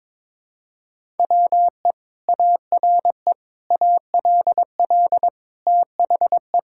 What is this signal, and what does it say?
Morse code, heard as a keyed tone, about 22 wpm, 700 Hz. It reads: WE ARE ALL THE
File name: practice / 22wpm / Sentences with 2-Word Phrases - 22wpm